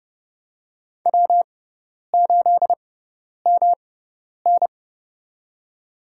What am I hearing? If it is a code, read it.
Morse code, heard as a keyed tone, about 30 wpm, spaced as for 10 wpm, 700 Hz. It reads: W8MN
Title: USA Call Signs - 4 Characters - 6x Character Spacing - 30wpm